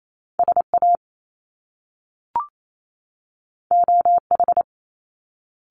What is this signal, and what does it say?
Morse code, heard as a keyed tone, about 28 wpm, 700 Hz. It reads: SA OH
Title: Top Two Letter Combinations in Words 28wpm